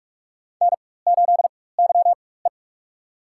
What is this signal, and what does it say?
Morse code, heard as a keyed tone, about 46 wpm, spaced as for 21 wpm, 700 Hz. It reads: N8YE
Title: USA Call Signs - 4 Characters - 4x Character Spacing - 45wpm